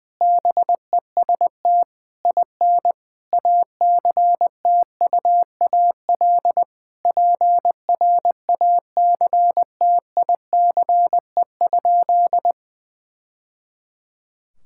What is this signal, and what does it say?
Morse code, heard as a keyed tone, about 21 wpm, 700 Hz. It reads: BEST IN ACTUAL PRACTICE?
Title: Sentences from Top 1000 Words - Review Part 5 - 20wpm